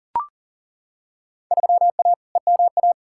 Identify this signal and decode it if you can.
Morse code, heard as a keyed tone, about 38 wpm, 700 Hz. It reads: 3A EMA
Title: Field Day Exchanges - 40wpm